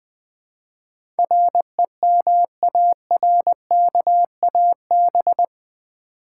Morse code, heard as a keyed tone, about 20 wpm, 700 Hz. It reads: REMARKAB
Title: Build A Word - 4 Letters At A Time - 10-Letter Words - 20wpm